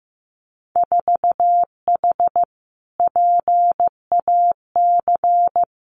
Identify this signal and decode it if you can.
Morse code, heard as a keyed tone, about 15 wpm, 700 Hz. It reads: 4H PAC
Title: Winter Field Day Exchanges Encore 15wpm